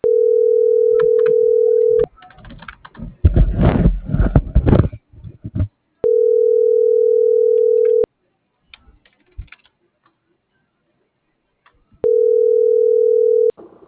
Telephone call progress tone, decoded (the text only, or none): ringback tone